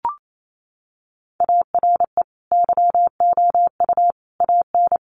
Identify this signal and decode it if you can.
Morse code, heard as a keyed tone, about 28 wpm, 700 Hz. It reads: ARE YOU AN
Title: Sentences from Top 1200 Words 28wpm